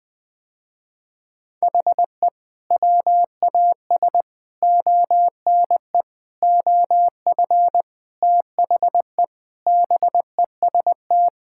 Morse code, heard as a keyed tone, about 21 wpm, 700 Hz. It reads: HE WAS ONE OF THE BEST
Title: Sentences with 5-Word Phrases - Spoken Beginning - 20wpm